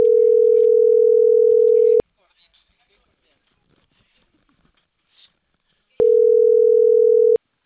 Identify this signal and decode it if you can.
Ringback tone (North America), 440 + 480 Hz, with speech.
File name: CENTRAL CAEA / out agcaea06